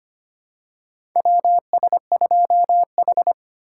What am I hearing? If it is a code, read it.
Morse code, heard as a keyed tone, about 25 wpm, 700 Hz. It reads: WS2H
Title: USA Call Signs - 4 Characters - 4x Character Spacing - 25wpm